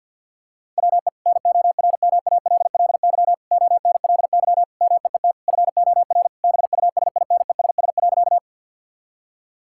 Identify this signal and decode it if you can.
Morse code, heard as a keyed tone, about 50 wpm, 700 Hz. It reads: WE NORMALLY ONLY MEET FOR BUSINESS.
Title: Sentences from Top 1000 Words - Review Part 6 - 50wpm